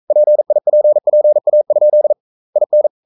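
Morse code, heard as a keyed tone, about 42 wpm, 600 Hz. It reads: PIPPA? IN
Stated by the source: Morse code sending of an English novel at 42 wpm.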